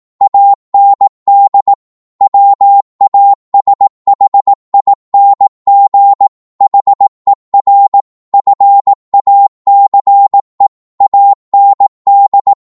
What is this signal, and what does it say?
Morse code, heard as a keyed tone, about 18 wpm, 800 Hz. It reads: AND WASHING HER FACE AND
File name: Morse / psg-18wpm